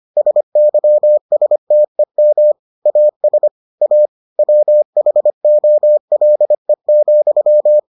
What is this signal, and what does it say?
Morse code, heard as a keyed tone, about 25 wpm, 600 Hz. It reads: SYSTEM AS A WHOLE,